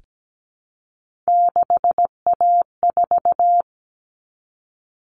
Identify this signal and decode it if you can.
Morse code, heard as a keyed tone, about 17 wpm, 700 Hz. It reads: 6A4
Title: All Cut Numbers - 3-Digit 17wpm